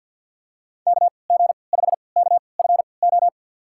Morse code, heard as a keyed tone, about 53 wpm, spaced as for 29 wpm, 700 Hz. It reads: KG5KFO
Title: USA Call Signs - 6 Characters - 3x Character Spacing - 50wpm